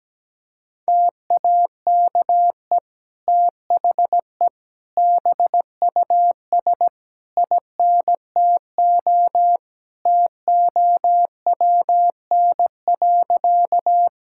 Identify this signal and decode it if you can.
Morse code, heard as a keyed tone, about 17 wpm, 700 Hz. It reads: TAKE THE BUS INTO TOWN.
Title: Sentences from Top 1000 Words - Review Part 6 - 17wpm